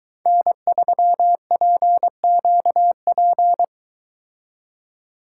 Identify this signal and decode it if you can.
Morse code, heard as a keyed tone, about 23 wpm, 700 Hz. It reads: N3PQP